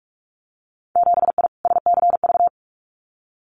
Morse code, heard as a keyed tone, about 46 wpm, 700 Hz. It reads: 7I SCV
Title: Winter Field Day Exchanges Encore 45wpm